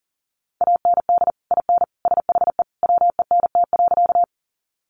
Morse code, heard as a keyed tone, about 40 wpm, 700 Hz. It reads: AND IN SHE WENT.
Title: Sentences from Top 200 Words 40wpm